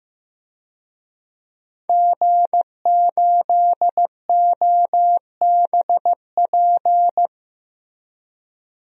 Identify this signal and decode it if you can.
Morse code, heard as a keyed tone, about 15 wpm, 700 Hz. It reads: G8OBP
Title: DXCC-100 Call Signs - 5 Characters - 15wpm